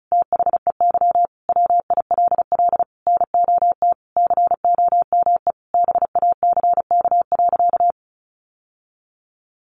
Morse code, heard as a keyed tone, about 34 wpm, 700 Hz. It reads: THEY WILL NOT COME BACK.
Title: Sentences from Top 1300 Words 35wpm